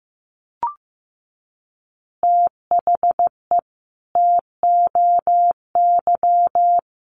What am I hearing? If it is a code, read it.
Morse code, heard as a keyed tone, about 15 wpm, 700 Hz. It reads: THE TOY
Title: Easy Phrases 15wpm